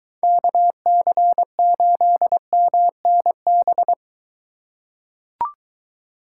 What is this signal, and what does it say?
Morse code, heard as a keyed tone, about 21 wpm, 700 Hz. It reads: KC8MNB